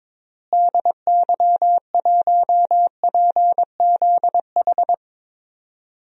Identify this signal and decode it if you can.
Morse code, heard as a keyed tone, about 21 wpm, 700 Hz. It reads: DY1PZH
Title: DXCC-100 Call Signs - 6 Characters - Repeated 3x - 22wpm